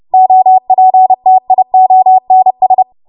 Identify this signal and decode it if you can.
Morse code, heard as a keyed tone, about 30 wpm, 750 Hz. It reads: OPTIONS
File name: Morse / 30WPM